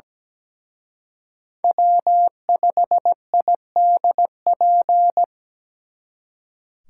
Morse code, heard as a keyed tone, about 21 wpm, 700 Hz. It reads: W5IDP
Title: Speed Racing - USA Call Signs 17wpm